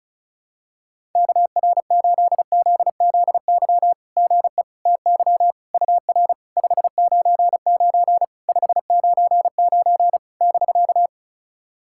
Morse code, heard as a keyed tone, about 36 wpm, 660 Hz. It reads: KR8ZZY GE TY UR 599 599 <BK>
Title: SOTA QSO Practice 35wpm